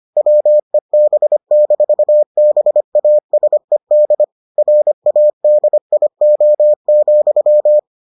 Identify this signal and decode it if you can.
Morse code, heard as a keyed tone, about 25 wpm, 600 Hz. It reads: WEB-BASED RADIO,